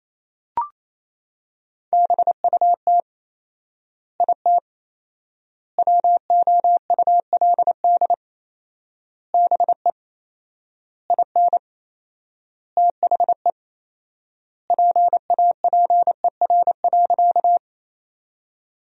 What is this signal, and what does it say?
Morse code, heard as a keyed tone, about 28 wpm, 700 Hz. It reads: BUT IT WOULD BE IN THE PAPER.
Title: 4x Word-Spacing - Sentences from Top 300 Words - 28wpm